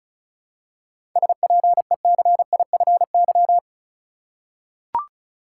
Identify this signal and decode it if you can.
Morse code, heard as a keyed tone, about 34 wpm, 700 Hz. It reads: SPECIFY
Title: Seven Letter Words - 35wpm